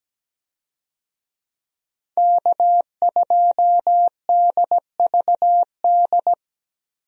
Morse code, heard as a keyed tone, about 17 wpm, 700 Hz. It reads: K2DVD
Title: USA Call Signs - 5 Characters - 17wpm